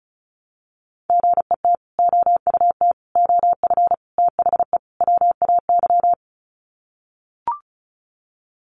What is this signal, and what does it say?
Morse code, heard as a keyed tone, about 34 wpm, 700 Hz. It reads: GET OUT OF THE WAY
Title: Sets of 5 Words Encore 35wpm